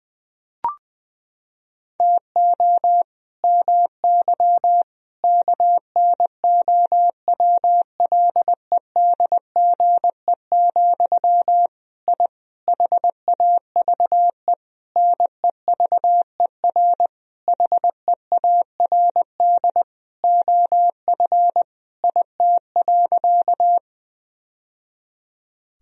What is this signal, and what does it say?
Morse code, heard as a keyed tone, about 21 wpm, 700 Hz. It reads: TO MY KNOWLEDGE, I HAVE NEVER HEARD OF IT.
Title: Sentences from Top 1000 Words - Review Part 3 - 20wpm